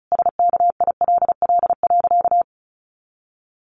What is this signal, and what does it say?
Morse code, heard as a keyed tone, about 34 wpm, 700 Hz. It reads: SKILL.
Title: Sentences from Top 1200 Words 35wpm